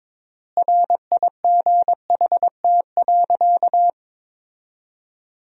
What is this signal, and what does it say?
Morse code, heard as a keyed tone, about 21 wpm, 700 Hz. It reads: RIGHT.
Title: 3x Word-Spacing - Sentences from Top 500 Words - 22wpm